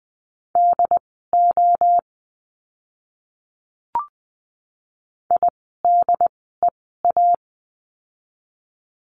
Morse code, heard as a keyed tone, about 20 wpm, spaced as for 15 wpm, 700 Hz. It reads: DO IDEA
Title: 108 Lesson 18f - Letter m - Words 20wpm